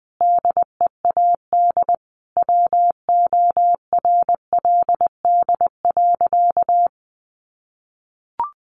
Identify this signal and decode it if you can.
Morse code, heard as a keyed tone, about 20 wpm, 700 Hz. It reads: DEAD WORLD.
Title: Sentences from Top 1500 Words 20wpm